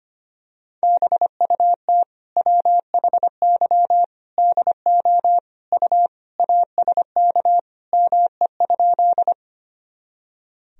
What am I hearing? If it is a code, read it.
Morse code, heard as a keyed tone, about 25 wpm, 700 Hz. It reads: BUT WHY DO U ASK ME?
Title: Sentences from Top 200 Words and Abbreviations 25wpm